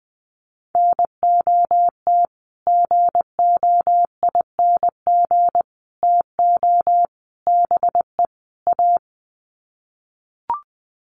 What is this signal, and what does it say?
Morse code, heard as a keyed tone, about 20 wpm, 700 Hz. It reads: NOT GOING TO BE A
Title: Sets of 5 Words Encore 20wpm